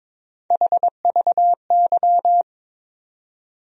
Morse code, heard as a keyed tone, about 22 wpm, 700 Hz. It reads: HVY
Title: USA Call Sign Postfixes - 3 Letters 22wpm